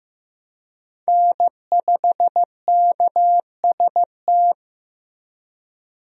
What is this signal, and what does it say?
Morse code, heard as a keyed tone, about 16 wpm, 700 Hz. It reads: N5KST
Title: USA Call Signs - 5 Characters - Repeated 2x - 15wpm